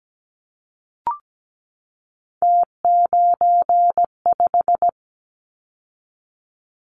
Morse code, heard as a keyed tone, about 17 wpm, 700 Hz. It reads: T95
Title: Callsign Prefixes for Top 100 DXCC - 17wpm